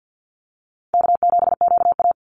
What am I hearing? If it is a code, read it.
Morse code, heard as a keyed tone, about 48 wpm, 700 Hz. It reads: K7QA
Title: Call Signs - 50wpm